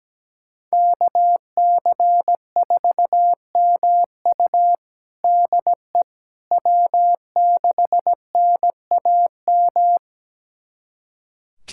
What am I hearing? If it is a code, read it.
Morse code, heard as a keyed tone, about 17 wpm, 700 Hz. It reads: KC4MU DE W6NAM
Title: USA Callsign DE Callsign - 5 Characters - 17wpm